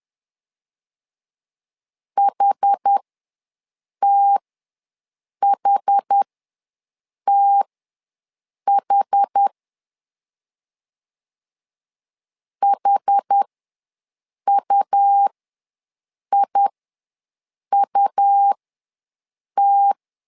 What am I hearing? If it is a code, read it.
Morse code, heard as a keyed tone, about 10 wpm, spaced as for 6 wpm, 800 Hz. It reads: HTHTH HUIUT
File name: LEZIONI / 2 UHDN / 2 UH ws 10wpm 2029